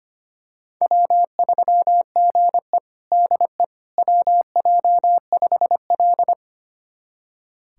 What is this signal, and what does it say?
Morse code, heard as a keyed tone, about 25 wpm, 700 Hz. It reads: W3GE DE WJ5L
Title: USA Callsign DE Callsign - 4 Characters - 25wpm